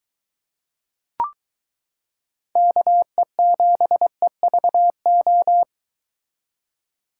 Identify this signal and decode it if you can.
Morse code, heard as a keyed tone, about 22 wpm, 700 Hz. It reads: KE7EVO